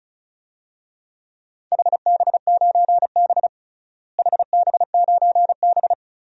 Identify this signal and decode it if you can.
Morse code, heard as a keyed tone, about 36 wpm, 700 Hz. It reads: HB9B HB9B
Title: DXCC-100 Call Signs - 4 Characters - Repeated 2x - 35wpm